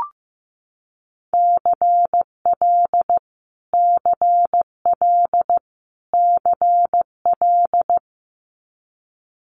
Morse code, heard as a keyed tone, about 15 wpm, 700 Hz. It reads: CL CL CL
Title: Multi-letter Phonemes Repeated 3x 15wpm